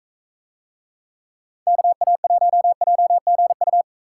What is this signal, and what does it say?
Morse code, heard as a keyed tone, about 40 wpm, 700 Hz. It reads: KA1JGU